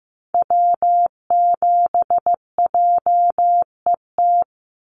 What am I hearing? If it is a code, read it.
Morse code, heard as a keyed tone, about 15 wpm, 700 Hz. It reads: W7JET
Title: NA SOTA Activator Callsigns 15wpm